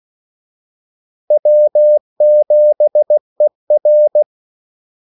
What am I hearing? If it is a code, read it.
Morse code, heard as a keyed tone, about 16 wpm, 600 Hz. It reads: W7ER
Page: Kaldesignaler 16 wpm